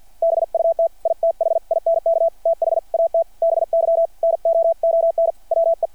Morse code, heard as a keyed tone, about 50 wpm, 660 Hz. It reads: BUT I THINK THAT BY NOON WE
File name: archived / CRSnet